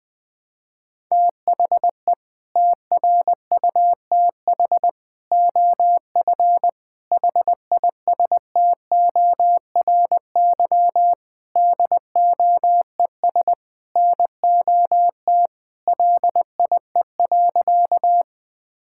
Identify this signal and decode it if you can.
Morse code, heard as a keyed tone, about 20 wpm, 700 Hz. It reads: THE TRUTH OF HISTORY DOES NOT LIE.
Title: Sentences from Top 1600 Words 20wpm